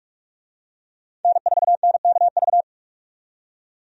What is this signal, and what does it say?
Morse code, heard as a keyed tone, about 44 wpm, 700 Hz. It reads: N4NKV
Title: USA Call Signs - 5 Characters - 45wpm